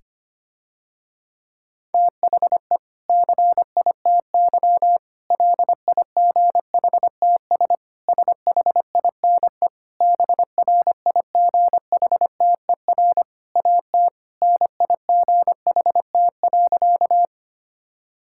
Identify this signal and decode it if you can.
Morse code, heard as a keyed tone, about 25 wpm, 700 Hz. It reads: THE CITY LIGHTS SHINE BRIGHTER AT NIGHT.
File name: practice / 25wpm / Difficult Sentences with Hints - 25wpm